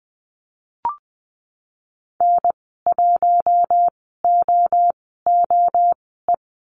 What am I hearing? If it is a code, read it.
Morse code, heard as a keyed tone, about 20 wpm, spaced as for 14 wpm, 700 Hz. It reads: N1OOE
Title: 042 Lesson 8g - Number 1 - Call Signs 20wpm